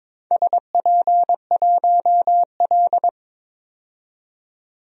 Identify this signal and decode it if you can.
Morse code, heard as a keyed tone, about 22 wpm, 690 Hz. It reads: SP1L
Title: DXCC-100 Call Signs - 4 Characters - 22wpm